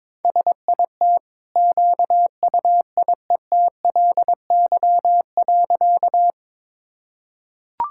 Morse code, heard as a keyed tone, about 22 wpm, 690 Hz. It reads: SIT QUIETLY.
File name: practice / 22wpm / Easy Sentences with Hints - 22wpm